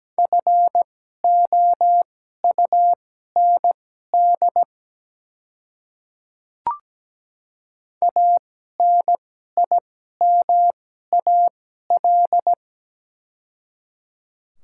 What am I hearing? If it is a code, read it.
Morse code, heard as a keyed tone, about 17 wpm, spaced as for 12 wpm, 700 Hz. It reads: FOUND ANIMAL
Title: 2x Character Spacing - Top 300 Words - 17wpm